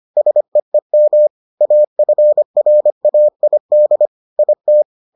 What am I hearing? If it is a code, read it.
Morse code, heard as a keyed tone, about 25 wpm, 600 Hz. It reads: SEEM AFRAID IT